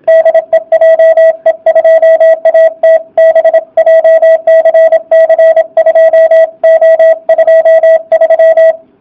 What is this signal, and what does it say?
Morse code, heard as a keyed tone, about 25 wpm, 660 Hz. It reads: DE JE2ATB JCC 2O23
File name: jh8cbh / 2019cw / sp32- w